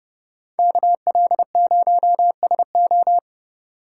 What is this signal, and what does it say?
Morse code, heard as a keyed tone, about 30 wpm, 700 Hz. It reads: KL0SO